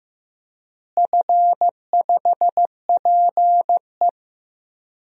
Morse code, heard as a keyed tone, about 15 wpm, 700 Hz. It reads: F5PE
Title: DXCC-100 Call Signs - 4 Characters - Repeated 2x - 15wpm